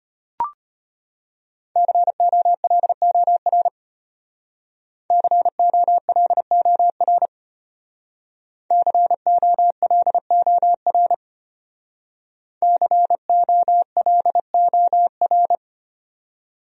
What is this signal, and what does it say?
Morse code, heard as a keyed tone, about 33 wpm, 700 Hz. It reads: COLOR COLOR COLOR COLOR